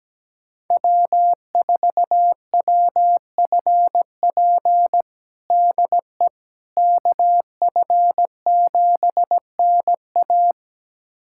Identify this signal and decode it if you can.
Morse code, heard as a keyed tone, about 17 wpm, 700 Hz. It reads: W4WFP DE KF7NA
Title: USA Callsign DE Callsign - 5 Characters - 17wpm